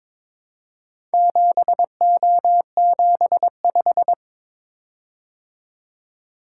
Morse code, heard as a keyed tone, about 22 wpm, 700 Hz. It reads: 7O75
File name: practice / 22wpm / Four Letter-Number Combinations - 22wpm